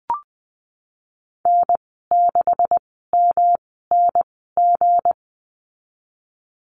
Morse code, heard as a keyed tone, about 20 wpm, spaced as for 15 wpm, 700 Hz. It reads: N6MNG
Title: 179 Lesson 29g - Slash - Call Signs 20wpm